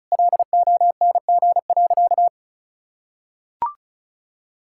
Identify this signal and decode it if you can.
Morse code, heard as a keyed tone, about 36 wpm, 700 Hz. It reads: LONG.
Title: Sentences from Top 1600 Words 35wpm